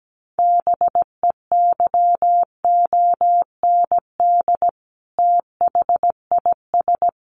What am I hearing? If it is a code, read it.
Morse code, heard as a keyed tone, about 17 wpm, 700 Hz. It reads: BEYOND THIS
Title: A Sound Of Thunder 17wpm